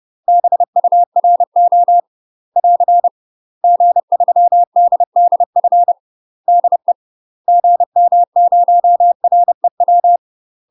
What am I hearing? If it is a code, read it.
Morse code, heard as a keyed tone, about 30 wpm, 700 Hz. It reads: BURO <AR> G3DDF DE GM0REW
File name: UK MASTER QSOs TRAINING / 30WPM / UK MASTER QSOs 0034